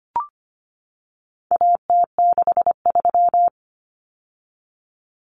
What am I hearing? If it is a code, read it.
Morse code, heard as a keyed tone, about 25 wpm, 700 Hz. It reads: AT63
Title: Common Cut Numbers - 4-Digit 25wpm